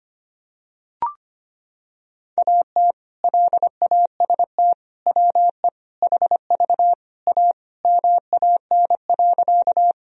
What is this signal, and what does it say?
Morse code, heard as a keyed tone, about 25 wpm, 700 Hz. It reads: AT LAST WE HV A MAN.
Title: Sentences from Top 300 Words and Abbreviations 25wpm